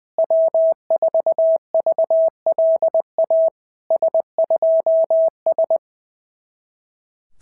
Morse code, heard as a keyed tone, about 20 wpm, 650 Hz. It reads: W4VLA S2S
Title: SOTA QSO Practice 20wpm